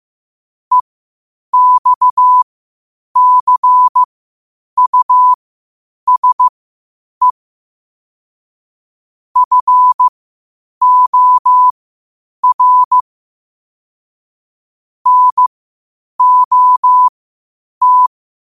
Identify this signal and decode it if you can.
Morse code, heard as a keyed tone, about 15 wpm, spaced as for 8 wpm, 1000 Hz. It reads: EXCUSE FOR NOT
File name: qotdfiles / 8wpm